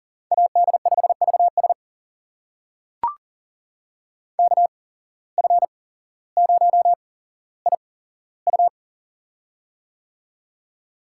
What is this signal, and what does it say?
Morse code, heard as a keyed tone, about 40 wpm, spaced as for 17 wpm, 700 Hz. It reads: AD5VS KF0IU
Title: USA Call Signs - 5 Characters - 8x Character Spacing - 40wpm